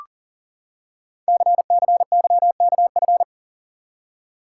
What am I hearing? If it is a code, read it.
Morse code, heard as a keyed tone, about 40 wpm, 690 Hz. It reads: CCYKF
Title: Code Groups - 5 Letters - 40wpm